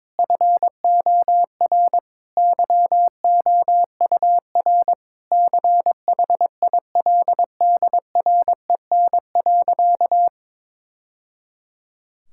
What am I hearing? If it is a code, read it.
Morse code, heard as a keyed tone, about 22 wpm, 700 Hz. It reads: FOR YOUR CHILDREN.
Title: Sentences from Top 1000 Words - Review Part 6 - 22wpm